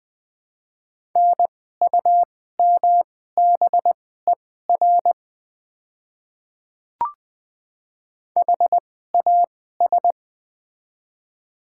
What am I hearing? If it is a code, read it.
Morse code, heard as a keyed tone, about 20 wpm, spaced as for 14 wpm, 700 Hz. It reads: NUMBER HAS
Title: 2x Character Spacing - Top 100 Words - 20wpm